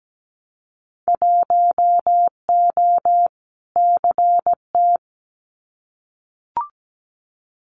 Morse code, heard as a keyed tone, about 17 wpm, 700 Hz. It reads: 1O CT
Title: Winter Field Day Exchanges 17wpm